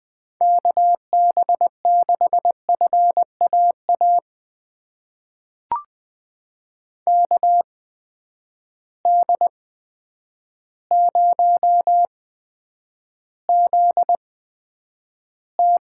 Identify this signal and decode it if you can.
Morse code, heard as a keyed tone, about 20 wpm, spaced as for 9 wpm, 700 Hz. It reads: KB6FAA KD0ZT
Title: USA Call Signs - 6 Characters - 8x Character Spacing - 20wpm